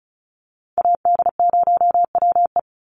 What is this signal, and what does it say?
Morse code, heard as a keyed tone, about 34 wpm, 700 Hz. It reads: AD0WE
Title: Call Signs - 35wpm